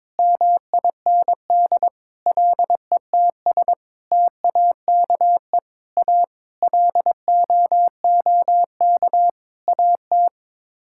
Morse code, heard as a keyed tone, about 22 wpm, 700 Hz. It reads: MIND LETS TAKE A LOOK AT
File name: Sets of 9 Words - 22wpm